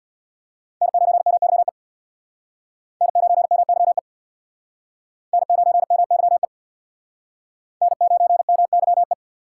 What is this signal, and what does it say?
Morse code, heard as a keyed tone, about 60 wpm, 700 Hz. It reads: N9MYE N9MYE N9MYE N9MYE
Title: Speed Racing - USA 1x3 Call Signs 50wpm